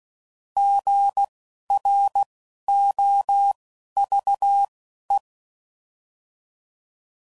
Morse code, heard as a keyed tone, about 16 wpm, spaced as for 11 wpm, 800 Hz. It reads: GROVE
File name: CW Code Groups @10WPM / fromSM's / Cities